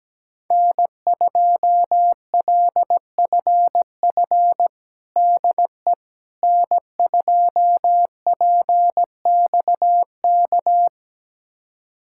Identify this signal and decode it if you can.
Morse code, heard as a keyed tone, about 17 wpm, 700 Hz. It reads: N2LFF DE N2PXK
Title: USA Callsign DE Callsign - 5 Characters - 17wpm